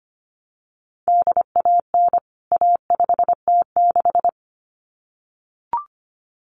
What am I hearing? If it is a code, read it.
Morse code, heard as a keyed tone, about 25 wpm, 700 Hz. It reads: DAN A5T6
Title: CWT Exchange 25wpm